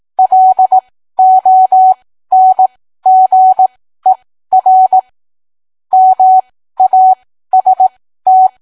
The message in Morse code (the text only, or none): LONGER MAST